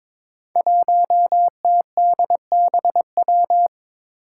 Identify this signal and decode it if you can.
Morse code, heard as a keyed tone, about 22 wpm, 700 Hz. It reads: 1TDBW
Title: Five Challenging Letter-Number Combinations 22wpm